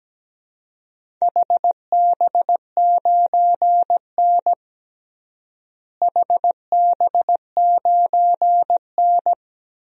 Morse code, heard as a keyed tone, about 17 wpm, 700 Hz. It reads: HB9N HB9N
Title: DXCC-100 Call Signs - 4 Characters - Repeated 3x - 17wpm